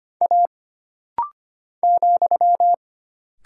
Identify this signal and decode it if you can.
Morse code, heard as a keyed tone, about 25 wpm, 700 Hz. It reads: A ,
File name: Single Character (Letter:Number:Punctuation:Prosign) - Rapid-Fire - 25wpm